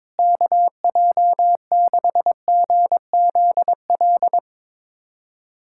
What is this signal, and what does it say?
Morse code, heard as a keyed tone, about 22 wpm, 700 Hz. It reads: KJ6GZL
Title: USA Callsign DE Callsign 22wpm